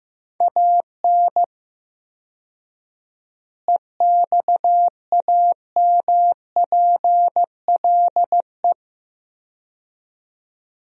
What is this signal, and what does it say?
Morse code, heard as a keyed tone, about 15 wpm, 700 Hz. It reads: AN EXAMPLE
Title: 4x Word-Spacing - Sentences from Top 400 Words - 15wpm